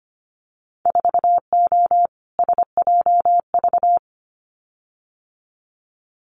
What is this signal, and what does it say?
Morse code, heard as a keyed tone, about 25 wpm, 700 Hz. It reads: 4O SJV
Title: Winter Field Day Exchanges Encore 25wpm